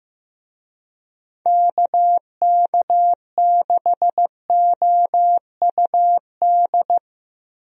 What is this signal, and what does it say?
Morse code, heard as a keyed tone, about 15 wpm, 700 Hz. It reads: KK6OUD
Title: USA Call Signs - 6 Characters - 15wpm